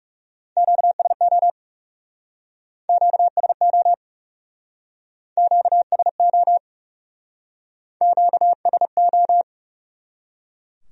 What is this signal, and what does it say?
Morse code, heard as a keyed tone, about 39 wpm, 700 Hz. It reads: QSO QSO QSO QSO